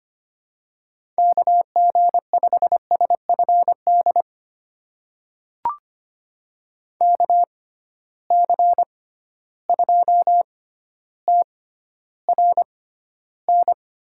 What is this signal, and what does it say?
Morse code, heard as a keyed tone, about 25 wpm, spaced as for 13 wpm, 700 Hz. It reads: KG5SFD KC2TRN
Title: USA Call Signs - 6 Characters - 6x Character Spacing - 25wpm